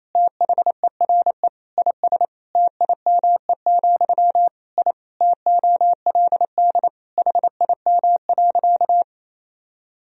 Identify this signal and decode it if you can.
Morse code, heard as a keyed tone, about 28 wpm, 700 Hz. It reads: THERE IS TIME, I TOLD HIM.
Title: Sentences from Top 1000 Words - Review Part 1 - 28wpm